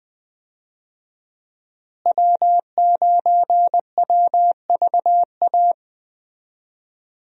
Morse code, heard as a keyed tone, about 21 wpm, 700 Hz. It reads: W9WVA
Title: USA Call Signs - 5 Characters - Repeated 2x - 20wpm